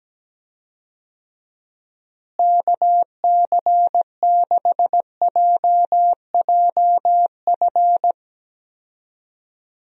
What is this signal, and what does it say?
Morse code, heard as a keyed tone, about 17 wpm, 700 Hz. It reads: KC6JJF
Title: USA Call Signs - 6 Characters - Repeated 2x - 17wpm